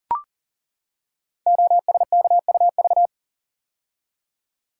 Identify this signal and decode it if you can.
Morse code, heard as a keyed tone, about 39 wpm, 690 Hz. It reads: OSKUV
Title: Code Groups - 5 Letters - 40wpm